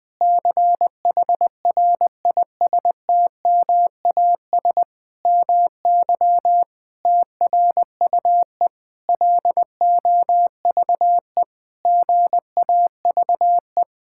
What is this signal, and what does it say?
Morse code, heard as a keyed tone, about 20 wpm, 700 Hz. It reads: CHRISTMAS MY TRUE LOVE GAVE
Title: Sentences with 8-Word Phrases - 20wpm